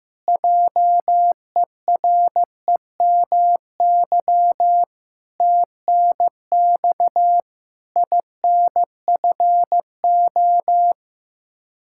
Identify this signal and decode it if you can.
Morse code, heard as a keyed tone, about 15 wpm, 700 Hz. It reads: JEREMY TNX INFO
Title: Practice QSO - Exchange 3 - More Info - 15wpm